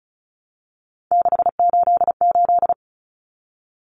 Morse code, heard as a keyed tone, about 34 wpm, 700 Hz. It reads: 688
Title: Numbers - 3-Digit 35wpm